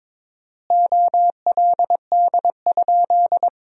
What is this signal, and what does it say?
Morse code, heard as a keyed tone, about 22 wpm, 700 Hz. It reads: OLD?
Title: 4x Word-Spacing - Sentences from Top 400 Words - 22wpm